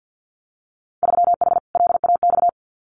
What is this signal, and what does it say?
Morse code, heard as a keyed tone, about 50 wpm, 700 Hz. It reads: 3H LAX
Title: Winter Field Day Exchanges 50wpm